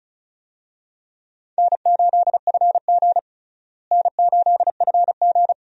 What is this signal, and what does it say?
Morse code, heard as a keyed tone, about 34 wpm, 700 Hz. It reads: N8FG N8FG
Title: USA Call Signs - 4 Characters - Repeated 2x - 35wpm